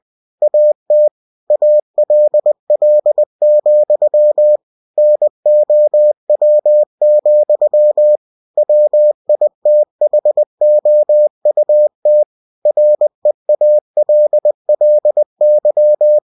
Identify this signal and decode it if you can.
Morse code, heard as a keyed tone, about 20 wpm, 600 Hz. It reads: AT ALL, NOW, WITHOUT REALLY
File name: cw-books / winnie-the-pooh / winnie-20wpm-page-0031